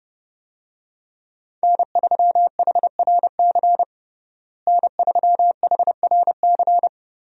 Morse code, heard as a keyed tone, about 30 wpm, 700 Hz. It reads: N3HRC N3HRC
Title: USA Call Signs - 5 Characters - Repeated 3x - 30wpm